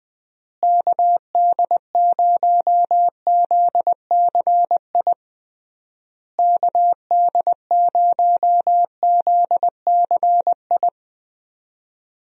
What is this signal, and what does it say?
Morse code, heard as a keyed tone, about 20 wpm, 700 Hz. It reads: KD0ZCI KD0ZCI
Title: USA Call Signs - 6 Characters - Repeated 3x - 20wpm